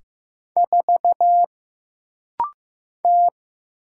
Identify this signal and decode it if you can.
Morse code, heard as a keyed tone, about 15 wpm, 700 Hz. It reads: 4 T
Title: Single Letter-Number Alt - Rapid-Fire 15wpm